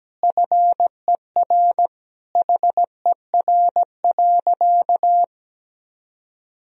Morse code, heard as a keyed tone, about 17 wpm, 700 Hz. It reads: FER HER.
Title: Sentences from Top 400 Words and Abbreviations 17wpm